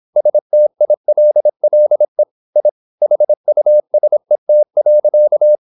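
Morse code, heard as a keyed tone, about 26 wpm, 600 Hz. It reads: STILLE I HUSET.
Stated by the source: High-speed (QRQ) Morse at 26 wpm.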